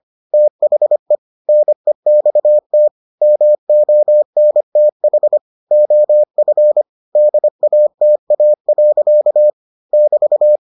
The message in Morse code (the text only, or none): THE NEXT MONTH OF DATA. <BT>